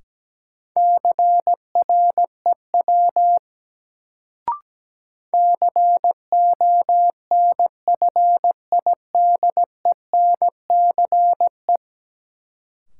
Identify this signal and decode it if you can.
Morse code, heard as a keyed tone, about 17 wpm, 700 Hz. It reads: CREW CONFIDENCE
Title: Sentences from Top 1500 Words - New Word Drill - 17wpm